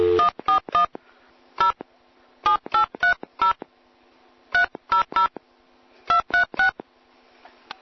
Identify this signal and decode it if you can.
DTMF, heard as telephone touch tones, tones of about 120 ms. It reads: 141 0 7830 300 236